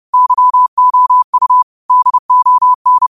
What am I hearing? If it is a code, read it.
Morse code, heard as a keyed tone, about 30 wpm, 1000 Hz. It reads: YOU DON
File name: qotdfiles / 30wpm